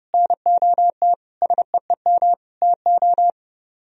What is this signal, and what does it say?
Morse code, heard as a keyed tone, about 30 wpm, 700 Hz. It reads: NOT SEEM TO